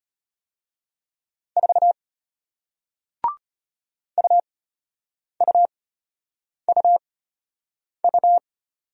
Morse code, heard as a keyed tone, about 33 wpm, spaced as for 8 wpm, 700 Hz. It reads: 4 UUUU